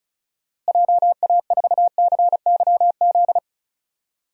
Morse code, heard as a keyed tone, about 36 wpm, 700 Hz. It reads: JA4CYZ
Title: DXCC-100 Call Signs - 6 Characters - Repeated 3x - 35wpm